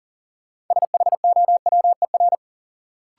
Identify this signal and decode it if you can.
Morse code, heard as a keyed tone, about 40 wpm, 700 Hz. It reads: SHOWER
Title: Sentences from Top 1600 Words - New Word Drill - 40wpm